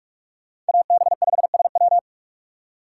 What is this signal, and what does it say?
Morse code, heard as a keyed tone, about 58 wpm, 700 Hz. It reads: AB5SW